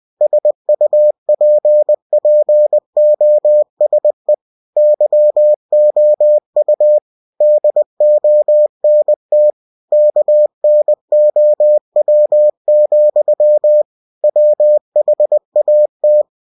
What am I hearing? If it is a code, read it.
Morse code, heard as a keyed tone, about 20 wpm, 600 Hz. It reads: SUPPOSE YOU DONT KNOW, WHAT
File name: cw-books / winnie-the-pooh / winnie-20wpm-page-0084